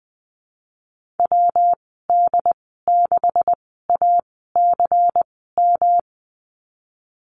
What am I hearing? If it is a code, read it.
Morse code, heard as a keyed tone, about 20 wpm, spaced as for 15 wpm, 700 Hz. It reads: WD6ACM
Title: 186 Lesson 30g - Letter b - Call Signs 20wpm